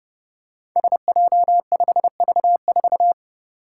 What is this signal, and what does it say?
Morse code, heard as a keyed tone, about 30 wpm, 700 Hz. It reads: SJ5V4
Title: Five Challenging Letter-Number Combinations 30wpm